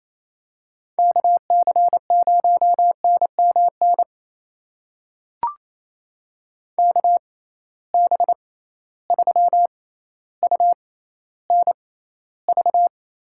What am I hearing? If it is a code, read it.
Morse code, heard as a keyed tone, about 28 wpm, spaced as for 14 wpm, 700 Hz. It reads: KC0NMN KB3UNV
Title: USA Call Signs - 6 Characters - 6x Character Spacing - 28wpm